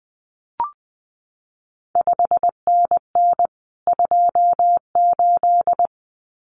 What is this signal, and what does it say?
Morse code, heard as a keyed tone, about 20 wpm, 700 Hz. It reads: 5NN 28
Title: CQ WW DX Contest Exchange Encore - 20wpm